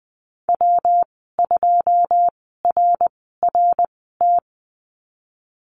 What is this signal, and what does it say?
Morse code, heard as a keyed tone, about 20 wpm, spaced as for 14 wpm, 700 Hz. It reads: W2RRT
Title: 161 Lesson 26g - Letter g - Call Signs 20wpm